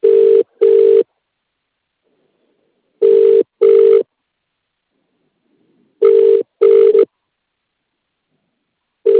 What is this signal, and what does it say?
Ringback tone (UK), 400 + 450 Hz.